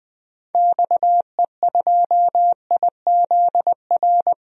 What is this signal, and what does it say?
Morse code, heard as a keyed tone, about 21 wpm, 700 Hz. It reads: XE2IZR
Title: DXCC-100 Call Signs - 6 Characters - Repeated 2x - 20wpm